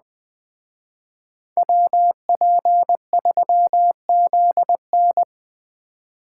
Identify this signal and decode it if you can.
Morse code, heard as a keyed tone, about 20 wpm, 700 Hz. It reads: WP3ZN
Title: Speed Racing - USA Call Signs 17wpm